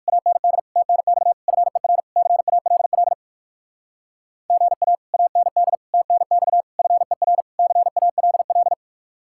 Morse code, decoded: AND TNX FER CALL GA AND TNX FER CALL

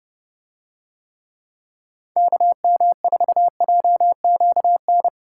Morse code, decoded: KM4JQN

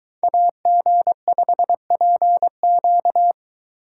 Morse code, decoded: AG5PQ

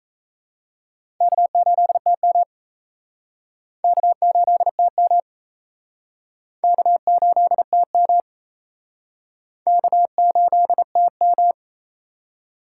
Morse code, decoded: K8TM K8TM K8TM K8TM

